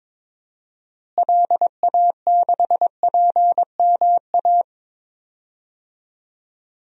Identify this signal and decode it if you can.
Morse code, heard as a keyed tone, about 22 wpm, 700 Hz. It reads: LA6PMA
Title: DXCC-100 Call Signs - 6 Characters - Repeated 2x - 22wpm